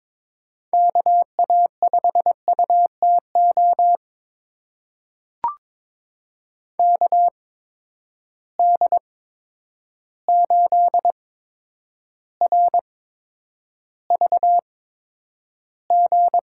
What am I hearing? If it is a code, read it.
Morse code, heard as a keyed tone, about 22 wpm, spaced as for 9 wpm, 700 Hz. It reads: KA5UTO KD8RVG
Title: USA Call Signs - 6 Characters - 8x Character Spacing - 22wpm